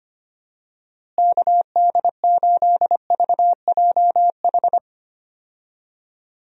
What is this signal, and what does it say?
Morse code, heard as a keyed tone, about 25 wpm, 700 Hz. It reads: KD8VJH
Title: USA Call Signs - 6 Characters - Repeated 2x - 25wpm